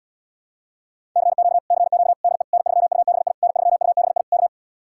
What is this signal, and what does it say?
Morse code, heard as a keyed tone, about 77 wpm, 700 Hz. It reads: CQ CQ DE N3UBI N3UBI K